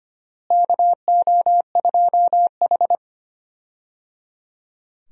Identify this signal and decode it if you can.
Morse code, heard as a keyed tone, about 25 wpm, 700 Hz. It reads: KO2H
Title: USA Call Signs - 4 Characters - Repeated 3x - 25wpm